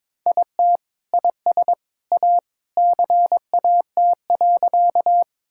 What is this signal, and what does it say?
Morse code, heard as a keyed tone, about 22 wpm, 700 Hz. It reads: IT IS A CAT.
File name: practice / 22wpm / Sentences with 3-Word Phrases - 22wpm